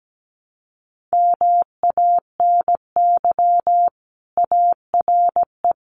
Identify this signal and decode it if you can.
Morse code, heard as a keyed tone, about 17 wpm, 700 Hz. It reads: MANY ARE
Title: Sentences from Top 1300 Words 17wpm